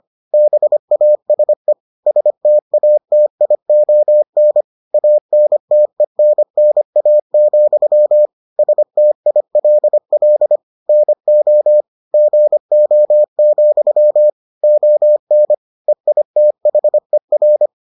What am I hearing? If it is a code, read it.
Morse code, heard as a keyed tone, about 25 wpm, 600 Hz. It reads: BASE STATION ANTENNA, STILL NO GO, ON EITHER